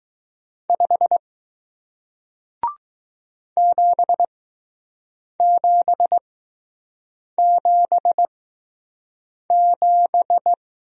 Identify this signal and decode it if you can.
Morse code, heard as a keyed tone, about 20 wpm, spaced as for 6 wpm, 700 Hz. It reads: 5 7777